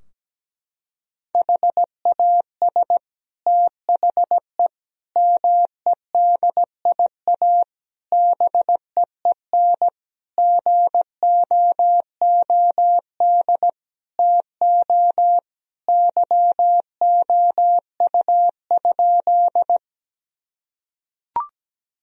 Morse code, heard as a keyed tone, about 17 wpm, 700 Hz. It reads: HAS THE MEDIA BEEN GOOD TO YOU?
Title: Sentences from Top 1000 Words - Review Part 4 - 17wpm